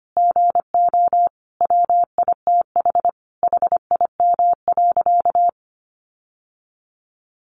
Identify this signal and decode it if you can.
Morse code, heard as a keyed tone, about 25 wpm, 700 Hz. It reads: GO WITH HIM.
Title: Sentences from Top 1300 Words 25wpm